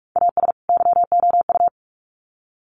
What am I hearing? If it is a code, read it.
Morse code, heard as a keyed tone, about 44 wpm, 700 Hz. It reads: AS YOU
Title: Sets of 2 Words 45wpm